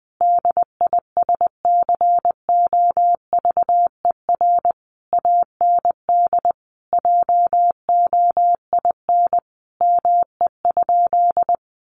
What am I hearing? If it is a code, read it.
Morse code, heard as a keyed tone, about 20 wpm, 700 Hz. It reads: DISCOVER AND JOIN ME?
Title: Sentences from Top 1400 Words 20wpm